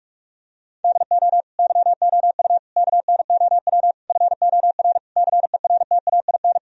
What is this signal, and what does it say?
Morse code, heard as a keyed tone, about 46 wpm, 700 Hz. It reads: DO YOU KNOW FOR CERTAIN
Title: Sentences from Top 1000 Words - Review Part 2 - 45wpm